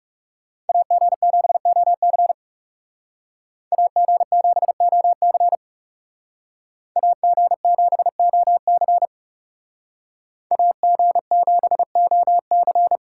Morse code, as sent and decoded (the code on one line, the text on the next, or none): .- --. --... --- -.-. / .- --. --... --- -.-. / .- --. --... --- -.-. / .- --. --... --- -.-.
AG7OC AG7OC AG7OC AG7OC